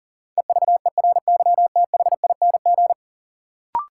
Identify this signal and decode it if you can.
Morse code, heard as a keyed tone, about 40 wpm, 700 Hz. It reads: EVERYTHING
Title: Sentences from Top 700 Words - New Word Drill - 40wpm